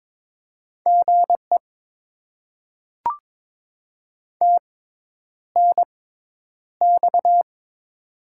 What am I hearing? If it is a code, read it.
Morse code, heard as a keyed tone, about 21 wpm, spaced as for 10 wpm, 700 Hz. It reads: GE TNX
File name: Top 20 QSO Elements - 6x Character Spacing - 22wpm